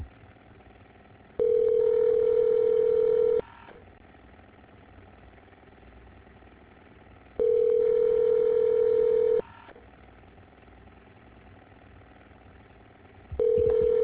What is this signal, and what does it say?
Ringback tone (North America), 440 + 480 Hz.